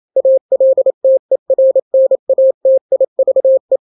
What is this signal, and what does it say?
Morse code, heard as a keyed tone, about 28 wpm, 530 Hz. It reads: ALTERNATIVE
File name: daily / 27wpm